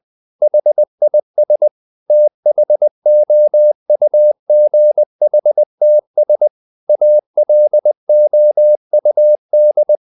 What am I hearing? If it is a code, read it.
Morse code, heard as a keyed tone, about 20 wpm, 600 Hz. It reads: HIS THOUGHTS ALOUD
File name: the old man and the sea / the old man and the sea part 1 20wpm